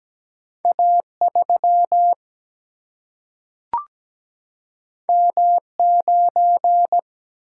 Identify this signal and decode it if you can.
Morse code, heard as a keyed tone, about 17 wpm, 700 Hz. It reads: A 3 M 9
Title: Single-Letter Single-Number 17wpm